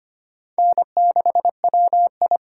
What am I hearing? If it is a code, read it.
Morse code, heard as a keyed tone, about 25 wpm, 700 Hz. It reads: N6WI
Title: USA Call Signs - 5 Characters - 8x Character Spacing - 25wpm